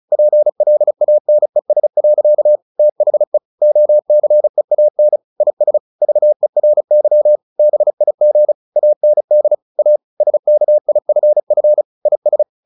Morse code, heard as a keyed tone, about 35 wpm, 600 Hz. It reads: PLANES. THE OCEAN IS VERY BIG AND A SKIFF IS
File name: the old man and the sea 2 35wpm